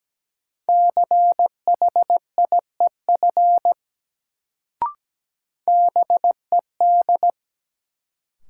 Morse code, heard as a keyed tone, about 17 wpm, 700 Hz. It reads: CHIEF BED
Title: Sets of 5 Words Encore - New Word Drill - 17wpm